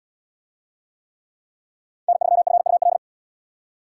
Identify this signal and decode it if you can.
Morse code, heard as a keyed tone, about 77 wpm, 700 Hz. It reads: N3FUL